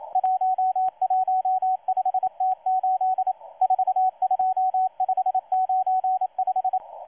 Morse code, heard as a keyed tone, about 27 wpm, 750 Hz, with radio noise. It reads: 115T8 42595
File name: M12-2016-07-07-1310utc-13926khz-error-restart